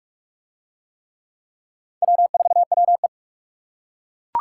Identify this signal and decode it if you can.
Morse code, heard as a keyed tone, about 41 wpm, 700 Hz. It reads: W4WE